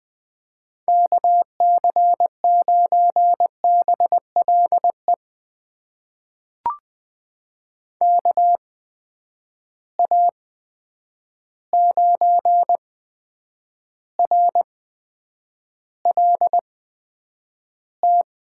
Morse code, heard as a keyed tone, about 20 wpm, spaced as for 9 wpm, 700 Hz. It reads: KC9BLE KA9RLT
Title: USA Call Signs - 8x Character Spacing - 20wpm